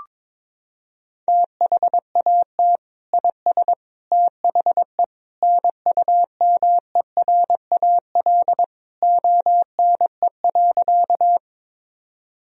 Morse code, heard as a keyed tone, about 22 wpm, 700 Hz. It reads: THAT IS THE NUMERAL ONE.
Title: Sentences from Top 1000 Words - Review Part 2 - 22wpm